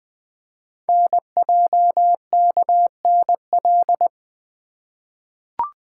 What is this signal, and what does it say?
Morse code, heard as a keyed tone, about 20 wpm, 700 Hz. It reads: NJKNL
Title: Code Groups - 5 Letter:Numbers - 20wpm